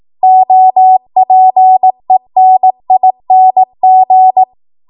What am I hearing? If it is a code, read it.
Morse code, heard as a keyed tone, about 18 wpm, 750 Hz. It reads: OPENING
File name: Archive / 18 WPM / 18WPM